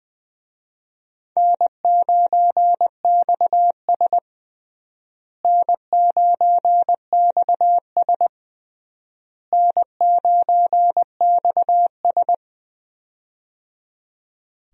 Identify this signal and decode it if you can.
Morse code, heard as a keyed tone, about 20 wpm, 700 Hz. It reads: N9XS N9XS N9XS
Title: USA Call Signs - 4 Characters - Repeated 3x - 20wpm